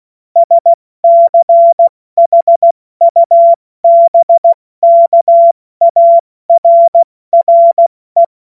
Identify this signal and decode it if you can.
Morse code, heard as a keyed tone, about 16 wpm, 680 Hz. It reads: SCHUBKARRE
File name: Morsekurs / Witze / Witz 13. Januar 2026 70 F80